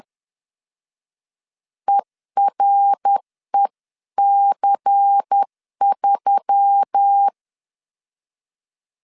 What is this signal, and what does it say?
Morse code, heard as a keyed tone, about 10 wpm, 800 Hz. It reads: EREC3
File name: CW / DET / 1 6 1500C S C 10wpm 4252